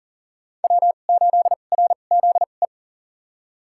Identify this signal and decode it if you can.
Morse code, heard as a keyed tone, about 38 wpm, spaced as for 28 wpm, 700 Hz. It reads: W8RZE
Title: USA Call Signs - 5 Characters - 2x Character Spacing - 40wpm